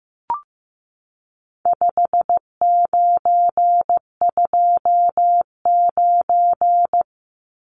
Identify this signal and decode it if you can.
Morse code, heard as a keyed tone, about 15 wpm, 700 Hz. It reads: 5929
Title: Numbers - 4-Digit 15wpm